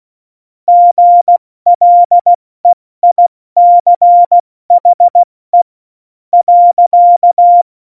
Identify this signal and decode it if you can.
Morse code, heard as a keyed tone, about 16 wpm, spaced as for 13 wpm, 700 Hz. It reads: GLEICHE.
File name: Morsekurs / Witze / Witz 02. Oktober 2018, 70 F80